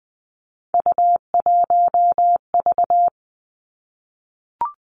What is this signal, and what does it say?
Morse code, heard as a keyed tone, about 20 wpm, 700 Hz. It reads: U1V
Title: Three Challenging Letter-Number Combinations 20wpm